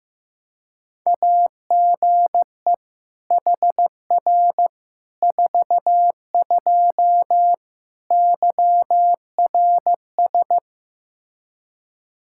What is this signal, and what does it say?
Morse code, heard as a keyed tone, about 16 wpm, 700 Hz. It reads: AGE HR 42 YRS
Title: Practice QSO - Exchange 3 - More Info - 15wpm